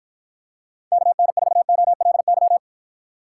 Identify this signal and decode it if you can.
Morse code, heard as a keyed tone, about 52 wpm, 700 Hz. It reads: KN4OLY